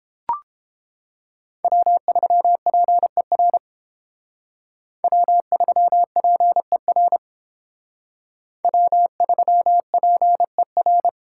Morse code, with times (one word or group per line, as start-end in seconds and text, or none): 1.6-3.6 W3PER
5.0-7.2 W3PER
8.6-11.1 W3PER